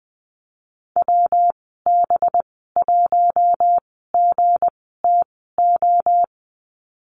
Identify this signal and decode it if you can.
Morse code, heard as a keyed tone, about 20 wpm, spaced as for 15 wpm, 700 Hz. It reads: WB1GTO
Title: 238 Lesson 38g - Letter q - Call Signs 20wpm